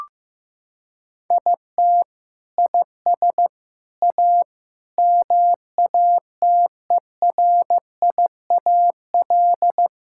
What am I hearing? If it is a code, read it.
Morse code, heard as a keyed tone, about 15 wpm, 700 Hz. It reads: IT IS A MATERIAL
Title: Sentences from Top 1000 Words - Review Part 5 - 15wpm